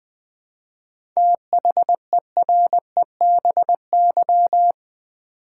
Morse code, heard as a keyed tone, about 20 wpm, 700 Hz. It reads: THEREBY
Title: 7 Letter Words - First 3 Letters Given - 20wpm